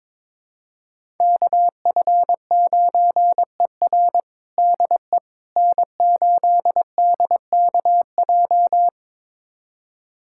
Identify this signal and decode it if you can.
Morse code, heard as a keyed tone, about 22 wpm, 700 Hz. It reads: KF9ER DE N8DKJ
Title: USA Callsign DE Callsign 22wpm